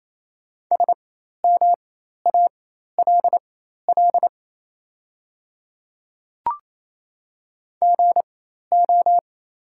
Morse code, heard as a keyed tone, about 28 wpm, spaced as for 13 wpm, 700 Hz. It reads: SMALL GO
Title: 4x Character Spacing - Top 500 Words - 28wpm